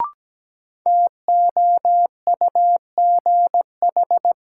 Morse code, heard as a keyed tone, about 17 wpm, 700 Hz. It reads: TOUGH
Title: Sentences from Top 1200 Words - New Word Drill Alt - 17wpm